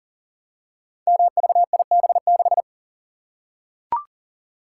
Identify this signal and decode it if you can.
Morse code, heard as a keyed tone, about 39 wpm, 690 Hz. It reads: MVIB6
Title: Code Groups - 5 Letter:Numbers - 40wpm